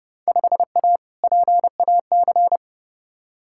Morse code, heard as a keyed tone, about 30 wpm, 700 Hz. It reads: 5A PAC
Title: Field Day Exchanges Encore - 30wpm